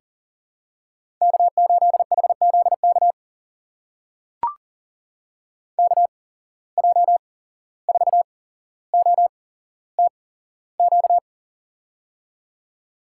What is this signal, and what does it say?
Morse code, heard as a keyed tone, about 40 wpm, spaced as for 16 wpm, 700 Hz. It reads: K8HZK KJ4OTQ
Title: USA Call Signs - 8x Character Spacing - 40wpm